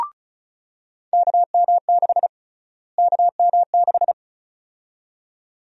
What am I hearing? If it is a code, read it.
Morse code, heard as a keyed tone, about 36 wpm, 700 Hz. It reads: KM6 KM6
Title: USA Call Sign Prefixes - Repeated 2x - 35wpm